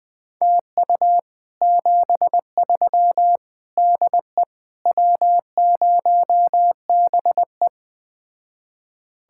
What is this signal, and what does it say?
Morse code, heard as a keyed tone, about 20 wpm, 700 Hz. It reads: TU 73 DE W0BE
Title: Speed-Racing Common QSO Phrases - 17wpm